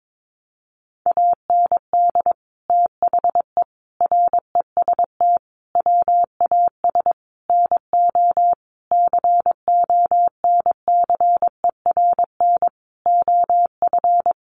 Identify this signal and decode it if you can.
Morse code, heard as a keyed tone, about 22 wpm, 700 Hz. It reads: AND THE REST WAS NO CONCERN OF
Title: Sentences from Top 1000 Words 22wpm